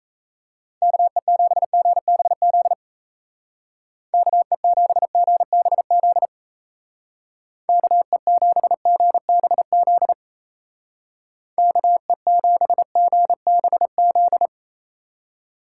Morse code, heard as a keyed tone, about 37 wpm, 700 Hz. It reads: KE7GBZ KE7GBZ KE7GBZ KE7GBZ